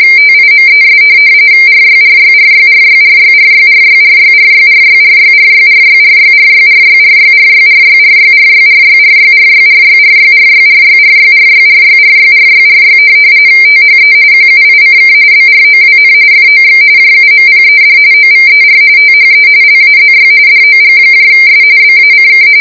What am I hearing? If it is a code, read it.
RTTY, about 45.45 baud. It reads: TIMES RYRYRYRYRYRYRYRYRYRYRYRYRYRYRYRYRYRYRYRYRYRYRYRYRYRYRYRYRYRYRYRYRYRY THE QUICK BROWN FOX JUMPED OVER THE LAZY DOGS BACK